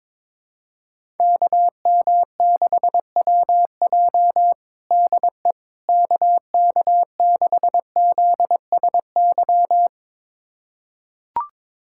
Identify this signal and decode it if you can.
Morse code, heard as a keyed tone, about 22 wpm, 700 Hz. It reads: KM6WJ DE KK6ZSY
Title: USA Callsign DE Callsign 22wpm